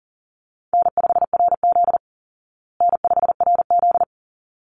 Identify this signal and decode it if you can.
Morse code, heard as a keyed tone, about 40 wpm, 700 Hz. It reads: N5RZ N5RZ
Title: Easy Call Signs Repeated 2x - 40wpm